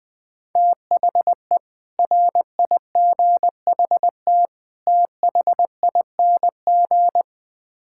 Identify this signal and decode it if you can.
Morse code, heard as a keyed tone, about 20 wpm, 700 Hz. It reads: THE RIGHT THING
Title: Build A Phrase - 5-Word Phrases - 20wpm